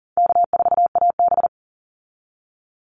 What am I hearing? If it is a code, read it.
Morse code, heard as a keyed tone, about 40 wpm, 700 Hz. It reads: K4AB
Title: Easy Call Signs Repeated 3x - 40wpm